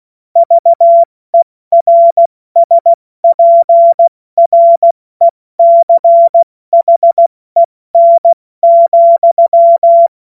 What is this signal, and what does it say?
Morse code, heard as a keyed tone, about 16 wpm, spaced as for 14 wpm, 680 Hz. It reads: VERSPRECHEN,
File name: Morsekurs / Witze / Witz 06. September 2022 70 F80